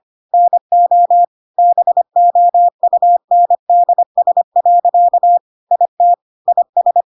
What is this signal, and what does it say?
Morse code, heard as a keyed tone, about 25 wpm, 700 Hz. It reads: NO BOUNDS. IT IS
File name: Pinocchio / Pinocchio ch5 25 25 ik1hge